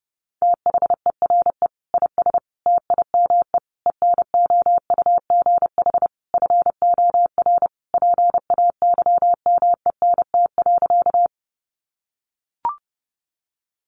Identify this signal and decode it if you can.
Morse code, heard as a keyed tone, about 30 wpm, 700 Hz. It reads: THERE IS TIME ENOUGH FOR PAYMENT.
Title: Sentences from Top 1200 Words 30wpm